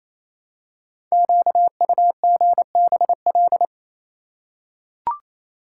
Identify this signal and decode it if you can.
Morse code, heard as a keyed tone, about 29 wpm, 700 Hz. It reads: QUGBL
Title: Code Groups - 5 Letters - 28wpm